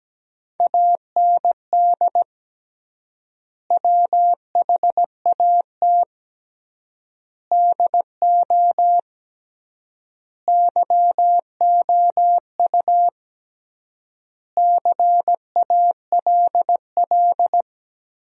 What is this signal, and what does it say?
Morse code, heard as a keyed tone, about 17 wpm, 700 Hz. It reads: AND WHAT DO YOU CALL
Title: 3x Word-Spacing - Sentences from Top 100 Words - 17wpm